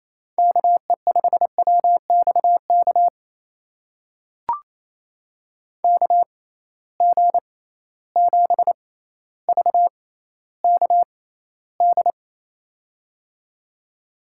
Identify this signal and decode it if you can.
Morse code, heard as a keyed tone, about 28 wpm, spaced as for 14 wpm, 700 Hz. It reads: KE5WXK KG7VKD
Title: USA Call Signs - 6 Characters - 6x Character Spacing - 28wpm